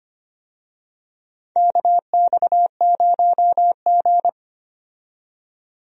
Morse code, heard as a keyed tone, about 25 wpm, 700 Hz. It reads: KX0G
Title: USA Call Signs - 4 Characters - 25wpm